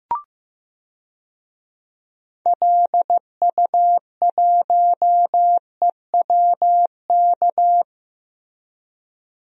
Morse code, heard as a keyed tone, about 15 wpm, 700 Hz. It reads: LU1EWK
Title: DXCC-100 Call Signs - 6 Characters - Repeated 3x - 15wpm